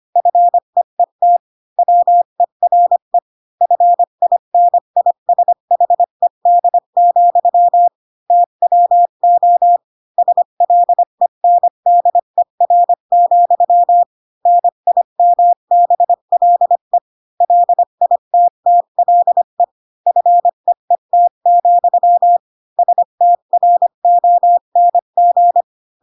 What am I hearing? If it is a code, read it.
Morse code, heard as a keyed tone, about 25 wpm, 700 Hz. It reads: FEET WERE FINISHED, TWO SLENDER, NIMBLE LITTLE FEET, STRONG
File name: Pinocchio / Pinocchio ch8 20 25 ik1hge